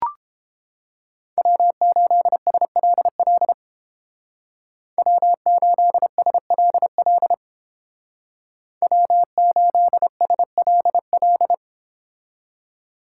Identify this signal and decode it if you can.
Morse code, heard as a keyed tone, about 29 wpm, 700 Hz. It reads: W8SLL W8SLL W8SLL